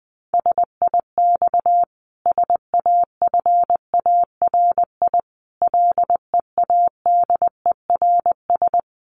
Morse code, heard as a keyed tone, about 20 wpm, 700 Hz. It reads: SIX SAFARI LEADERS
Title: A Sound Of Thunder 20wpm